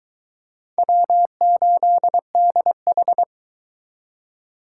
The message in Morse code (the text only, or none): W8DH